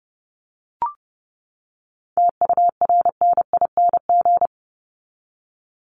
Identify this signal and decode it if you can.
Morse code, heard as a keyed tone, about 30 wpm, 700 Hz. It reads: TURNING
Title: ING Suffix - Top 100 Words with Base Words 30wpm